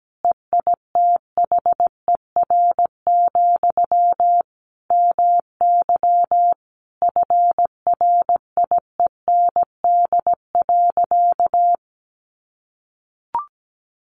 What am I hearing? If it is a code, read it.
Morse code, heard as a keyed tone, about 17 wpm, 700 Hz. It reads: EITHER, MY FRIEND.
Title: Sentences from Top 1100 Words 17wpm